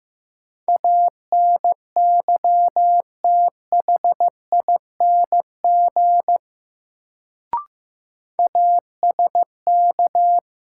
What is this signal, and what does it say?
Morse code, heard as a keyed tone, about 15 wpm, 700 Hz. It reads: ANYTHING ASK
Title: Sets of 4 Words - New Word Drill - 15wpm